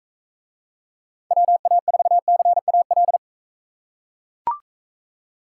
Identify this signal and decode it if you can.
Morse code, heard as a keyed tone, about 42 wpm, 700 Hz. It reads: WA4CAL